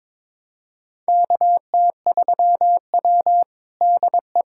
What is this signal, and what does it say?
Morse code, heard as a keyed tone, about 21 wpm, 700 Hz. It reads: KT3W DE
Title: USA Callsign DE Callsign - 4 Characters - 22wpm